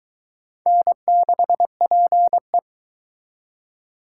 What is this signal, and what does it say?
Morse code, heard as a keyed tone, about 21 wpm, 700 Hz. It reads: N6PE